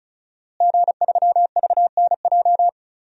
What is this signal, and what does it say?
Morse code, heard as a keyed tone, about 36 wpm, 700 Hz. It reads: G3VNJ
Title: DXCC-100 Call Signs - 5 Characters - Repeated 3x - 35wpm